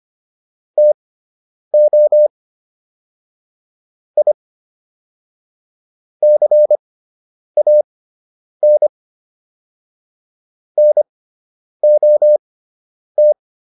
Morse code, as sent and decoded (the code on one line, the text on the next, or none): - --- / .. / -.-. .- -. / -. --- -
TO I CAN NOT